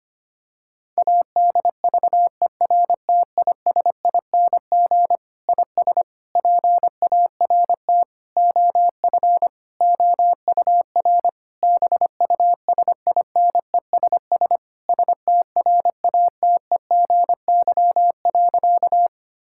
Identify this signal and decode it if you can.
Morse code, heard as a keyed tone, about 25 wpm, 700 Hz. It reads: ADVERTISING IS PART OF OUR BUSINESS STRATEGY.
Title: Sentences from Top 1000 Words - Review Part 6 - 25wpm